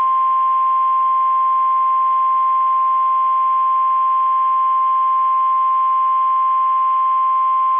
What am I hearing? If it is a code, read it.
Emergency Alert System (SAME) tones (the 1050 Hz attention tone). It is heard from 0.8 to 6.7 s.
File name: eas / KSDZKDJL / KSDZKDJL